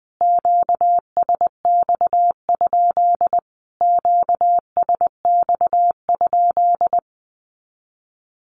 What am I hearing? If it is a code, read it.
Morse code, heard as a keyed tone, about 20 wpm, 700 Hz. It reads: QSX? QSX?
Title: Q-Codes Repeated 3x 20wpm